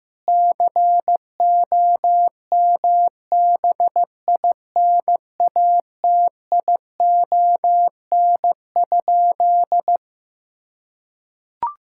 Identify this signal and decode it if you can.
Morse code, heard as a keyed tone, about 15 wpm, 700 Hz. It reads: COMBINATION?
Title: Sentences from Top 1000 Words - Review Part 6 - 15wpm